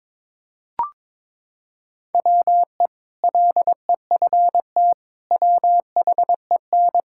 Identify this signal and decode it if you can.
Morse code, heard as a keyed tone, about 22 wpm, 700 Hz. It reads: WE LEFT WHEN
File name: practice / 22wpm / Sentences with 2-Word Phrases - 22wpm